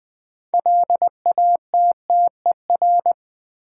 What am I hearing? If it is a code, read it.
Morse code, heard as a keyed tone, about 20 wpm, 700 Hz. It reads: LATTER
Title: Sentences from Top 1600 Words - New Word Drill - 20wpm